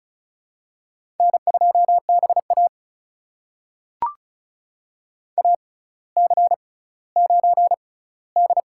Morse code, heard as a keyed tone, about 36 wpm, spaced as for 18 wpm, 700 Hz. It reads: N2BA AC9D
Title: USA Call Signs - 4 Characters - 6x Character Spacing - 35wpm